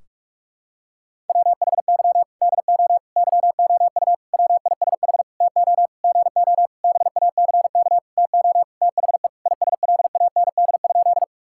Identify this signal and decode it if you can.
Morse code, heard as a keyed tone, about 42 wpm, 700 Hz. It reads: WHY DO YOU WISH TO GO BACK TO THE ISLAND?
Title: Sentences from Top 1600 Words 45wpm